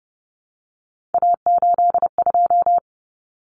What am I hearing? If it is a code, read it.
Morse code, heard as a keyed tone, about 30 wpm, 700 Hz. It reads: A82
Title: Common Cut Numbers - 3-Digit 30wpm